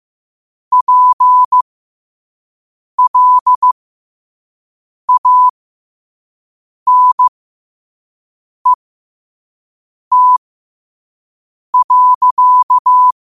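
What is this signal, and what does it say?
Morse code, heard as a keyed tone, about 15 wpm, spaced as for 5 wpm, 1000 Hz. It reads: PLANET.
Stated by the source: Morse code sent at 5 wpm.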